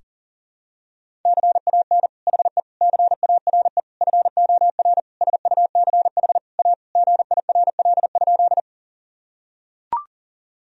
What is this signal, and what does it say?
Morse code, heard as a keyed tone, about 40 wpm, 700 Hz. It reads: CAN HE CARE FOR SUCH A GIRL?
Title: Sentences from Top 1000 Words - Review Part 1 - 40wpm